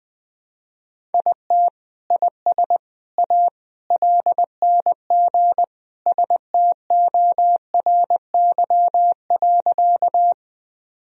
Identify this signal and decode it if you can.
Morse code, heard as a keyed tone, about 21 wpm, 700 Hz. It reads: IT IS A LNG STORY.
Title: Sentences from Top 300 Words and Abbreviations 20wpm